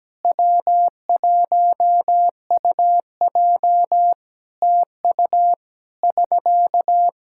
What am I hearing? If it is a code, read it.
Morse code, heard as a keyed tone, about 17 wpm, 700 Hz. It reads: W1UJ TU <SK>
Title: K1USN SST QSO Practice 17wpm